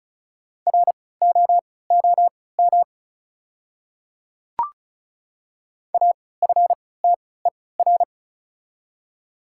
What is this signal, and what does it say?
Morse code, heard as a keyed tone, about 35 wpm, spaced as for 20 wpm, 700 Hz. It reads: ROOM AFTER
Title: 3x Character Spacing - Top 500 Words - 35wpm